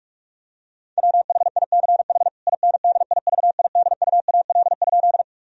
Morse code, heard as a keyed tone, about 41 wpm, 700 Hz. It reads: WHICH INDIVIDUAL?
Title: Sentences from Top 1000 Words - Review Part 4 - 45wpm